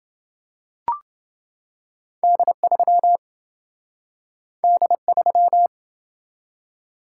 Morse code, heard as a keyed tone, about 27 wpm, 700 Hz. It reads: D3 D3